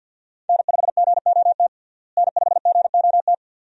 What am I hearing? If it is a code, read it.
Morse code, heard as a keyed tone, about 52 wpm, 700 Hz. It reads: N5GOT N5GOT
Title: USA Call Signs - 5 Characters - Repeated 2x - 50wpm